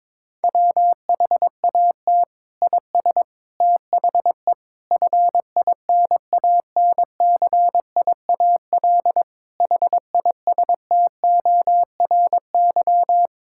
Morse code, heard as a keyed tone, about 22 wpm, 700 Hz. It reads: WHAT IS THE FINANCIAL HISTORY
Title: Sentences from Top 1000 Words - Review Part 5 - 22wpm